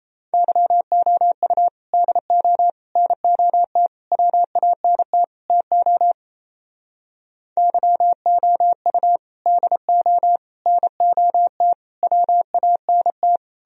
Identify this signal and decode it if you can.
Morse code, heard as a keyed tone, about 33 wpm, 700 Hz. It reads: YOU DO NOT WANT TO YOU DO NOT WANT
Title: Speed-Racing - Sets of 5 Words 28wpm